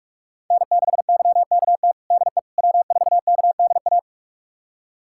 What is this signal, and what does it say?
Morse code, heard as a keyed tone, about 46 wpm, 700 Hz. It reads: N6YKT DE W4KDA
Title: USA Callsign DE Callsign - 5 Characters - 45wpm